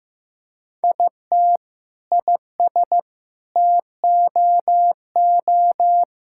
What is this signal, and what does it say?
Morse code, heard as a keyed tone, about 16 wpm, 700 Hz. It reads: IT IS TOO
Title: Sentences from Top 1600 Words 15wpm